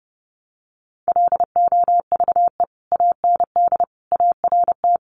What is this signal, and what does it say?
Morse code, heard as a keyed tone, about 30 wpm, 700 Hz. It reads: LOVE AND ART
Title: Sentences from Top 1100 Words 30wpm